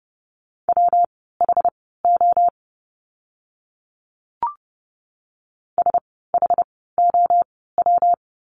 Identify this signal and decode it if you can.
Morse code, heard as a keyed tone, about 30 wpm, spaced as for 17 wpm, 700 Hz. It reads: WHO SHOW
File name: course / 30:10wpm / 230 Lesson 37f - Letter x - Words 30wpm